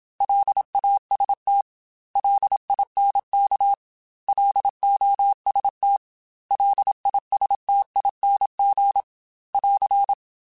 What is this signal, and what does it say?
Morse code, heard as a keyed tone, about 27 wpm, 800 Hz. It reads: LAST LINK LOST LISTING <AR>
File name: L Runs / L Runs mp3s / L Runs @25